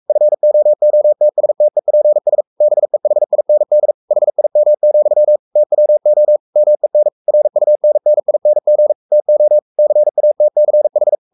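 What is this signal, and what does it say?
Morse code, heard as a keyed tone, about 44 wpm, 600 Hz. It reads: FOOTSTEPS BEHIND HIM, TWO MEN RUNNING TO CATCH
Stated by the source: Morse code at 43 wpm sending Ken Follett's novel Never in English, from CW med Gnister.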